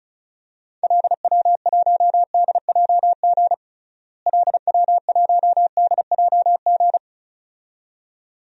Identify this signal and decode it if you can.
Morse code, heard as a keyed tone, about 34 wpm, 700 Hz. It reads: LW1DJG LW1DJG
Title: DXCC-100 Call Signs - 6 Characters - Repeated 2x - 35wpm